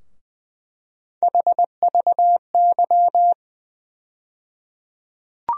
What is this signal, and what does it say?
Morse code, heard as a keyed tone, about 20 wpm, 700 Hz. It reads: HVY
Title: USA Call Sign Postfixes - 3 Letters 20wpm